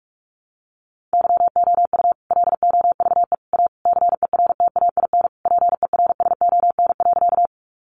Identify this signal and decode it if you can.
Morse code, heard as a keyed tone, about 44 wpm, 700 Hz. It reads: YOU LOVE A CERTAIN PERSON.
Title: Sentences from Top 500 Words 45wpm